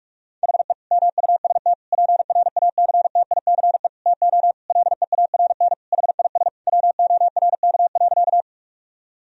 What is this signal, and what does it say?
Morse code, heard as a keyed tone, about 46 wpm, 700 Hz. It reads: HE MUST PRACTICE TO LEARN HIS WORK.
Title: Sentences from Top 1000 Words - Review Part 4 - 45wpm